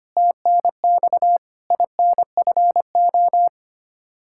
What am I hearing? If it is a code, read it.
Morse code, heard as a keyed tone, about 25 wpm, 700 Hz. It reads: TNX INFO
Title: Practice QSO - Exchange 3 - More Info - 25wpm